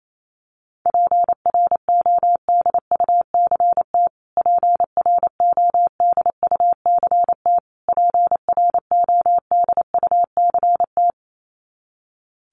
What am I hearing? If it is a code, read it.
Morse code, heard as a keyed tone, about 28 wpm, 700 Hz. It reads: PRODUCT PRODUCT PRODUCT
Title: Top 500 Words Repeated 28wpm